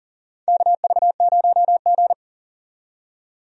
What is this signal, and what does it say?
Morse code, heard as a keyed tone, about 40 wpm, 700 Hz. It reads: KV0G
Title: USA Call Signs - 4 Characters - 40wpm